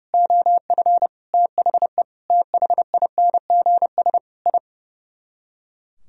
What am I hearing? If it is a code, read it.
Morse code, heard as a keyed tone, about 40 wpm, 700 Hz. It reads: OF THE THINGS I